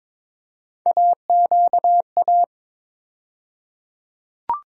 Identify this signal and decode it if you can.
Morse code, heard as a keyed tone, about 22 wpm, 700 Hz. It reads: AQA
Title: USA Call Sign Postfixes - 3 Letters 22wpm